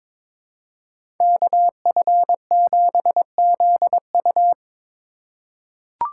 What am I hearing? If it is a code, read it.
Morse code, heard as a keyed tone, about 21 wpm, 700 Hz. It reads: KF7ZU
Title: USA Call Signs - 5 Characters - 4x Character Spacing - 22wpm